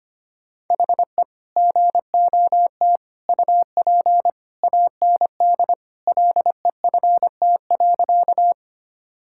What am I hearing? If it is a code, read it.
Morse code, heard as a keyed tone, about 25 wpm, 690 Hz. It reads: HE GOT UP AND LEFT.
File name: practice / 25wpm / Sentences with 2-Word Phrases - 25wpm